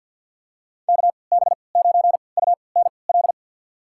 Morse code, heard as a keyed tone, about 50 wpm, spaced as for 28 wpm, 700 Hz. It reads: KB9UNL
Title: USA Call Signs - 6 Characters - 3x Character Spacing - 50wpm